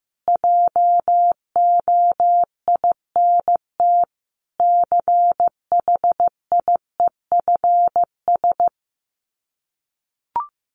Morse code, heard as a keyed tone, about 15 wpm, 700 Hz. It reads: JOINT CHIEFS
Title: Sets of 5 Words Encore 15wpm